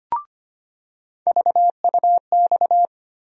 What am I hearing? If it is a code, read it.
Morse code, heard as a keyed tone, about 25 wpm, 700 Hz. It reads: VUX
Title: USA Call Sign Postfixes - 3 Letters 25wpm